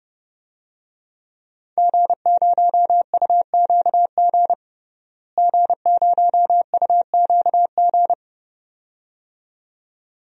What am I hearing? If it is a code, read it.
Morse code, heard as a keyed tone, about 30 wpm, 700 Hz. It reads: G0UQG G0UQG
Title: DXCC-100 Call Signs - 5 Characters - Repeated 2x - 30wpm